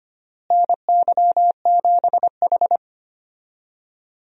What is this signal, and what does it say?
Morse code, heard as a keyed tone, about 25 wpm, 700 Hz. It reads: NY7H
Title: USA Call Signs - 4 Characters - Repeated 2x - 25wpm